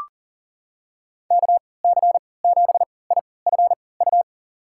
Morse code, heard as a keyed tone, about 40 wpm, spaced as for 23 wpm, 700 Hz. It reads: KC7IFU
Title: USA Call Signs - 6 Characters - 3x Character Spacing - 40wpm